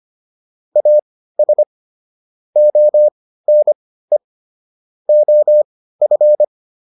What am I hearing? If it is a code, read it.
Morse code, heard as a keyed tone, about 25 wpm, spaced as for 15 wpm, 600 Hz. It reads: AS ONE OF